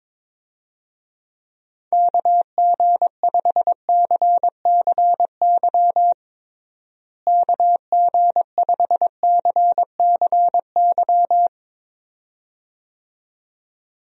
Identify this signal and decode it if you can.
Morse code, heard as a keyed tone, about 22 wpm, 700 Hz. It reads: KG5CCY KG5CCY
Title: USA Call Signs - 6 Characters - Repeated 2x - 22wpm